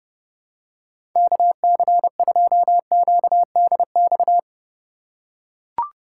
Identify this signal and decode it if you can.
Morse code, heard as a keyed tone, about 29 wpm, 700 Hz. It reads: KC2QDX
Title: USA Call Signs - 6 Characters - 4x Character Spacing - 30wpm